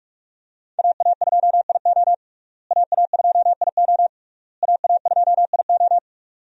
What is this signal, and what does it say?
Morse code, heard as a keyed tone, about 42 wpm, 700 Hz. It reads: AA2IO AA2IO AA2IO
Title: USA Call Signs - 5 Characters - Repeated 3x - 45wpm